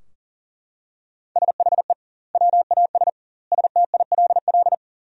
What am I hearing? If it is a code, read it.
Morse code, heard as a keyed tone, about 40 wpm, 700 Hz. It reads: SHE WAS STILL
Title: 2x Word-Spacing - Sentences from Top 500 Words - 40wpm